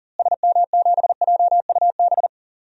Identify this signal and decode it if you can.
Morse code, heard as a keyed tone, about 40 wpm, 700 Hz. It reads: SM7JUB
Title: DXCC-100 Call Signs - 6 Characters - Repeated 2x - 40wpm